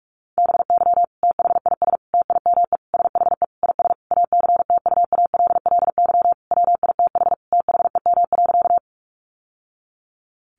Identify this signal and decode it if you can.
Morse code, heard as a keyed tone, about 44 wpm, 700 Hz. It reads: BY THIS TIME SHE IS ACTUALLY WITH THEM.
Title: Sentences from Top 700 Words 45wpm